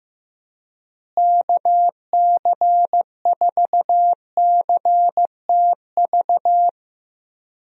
Morse code, heard as a keyed tone, about 15 wpm, 700 Hz. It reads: KC4CTV
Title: USA Call Signs - 6 Characters - Repeated 2x - 15wpm